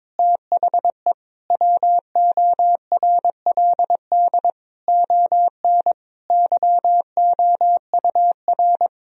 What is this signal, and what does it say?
Morse code, heard as a keyed tone, about 22 wpm, 700 Hz. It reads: THE WORLD ON YOUR
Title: Sentences from Top 1000 Words - Review Part 2 - 22wpm